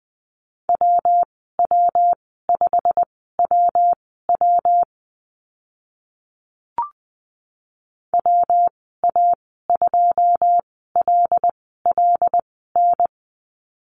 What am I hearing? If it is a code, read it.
Morse code, heard as a keyed tone, about 20 wpm, spaced as for 14 wpm, 700 Hz. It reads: WW5WW WA2LLN
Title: 122 Lesson 20g - Number 3 - Call Signs 20wpm